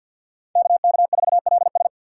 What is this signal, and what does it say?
Morse code, heard as a keyed tone, about 49 wpm, 700 Hz. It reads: KK4LS
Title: USA Call Signs - 5 Characters - 3x Character Spacing - 50wpm